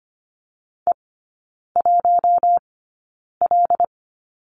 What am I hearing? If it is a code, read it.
Morse code, heard as a keyed tone, about 25 wpm, spaced as for 9 wpm, 700 Hz. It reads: E1L
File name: Lesson 14e - Number 2 - Three Characters 25wpm